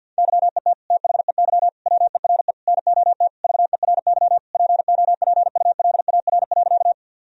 Morse code, heard as a keyed tone, about 50 wpm, 700 Hz. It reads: YET THEY WERE NOT VERY POPULAR.
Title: Sentences from Top 1000 Words - Review Part 3 - 50wpm